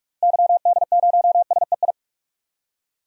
Morse code, heard as a keyed tone, about 46 wpm, 700 Hz. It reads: YD0SEI
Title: DXCC-100 Call Signs - 6 Characters - Repeated 2x - 45wpm